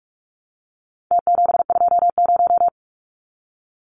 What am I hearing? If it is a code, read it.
Morse code, heard as a keyed tone, about 44 wpm, 700 Hz. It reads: T720
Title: Numbers - Contest Style - 4-Digit 45wpm